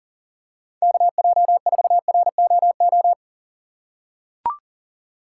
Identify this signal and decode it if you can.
Morse code, heard as a keyed tone, about 40 wpm, 700 Hz. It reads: KJ4ROO
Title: USA Call Signs - 6 Characters - 3x Character Spacing - 40wpm